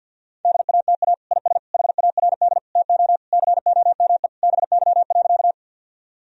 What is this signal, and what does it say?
Morse code, heard as a keyed tone, about 49 wpm, 700 Hz. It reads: DATA IS HARD TO COME BY.
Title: Sentences from Top 1000 Words - Review Part 3 - 50wpm